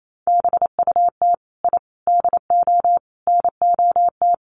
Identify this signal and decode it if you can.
Morse code, heard as a keyed tone, about 28 wpm, 700 Hz. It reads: BUT I DO NOT
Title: Sets of 5 Words 28wpm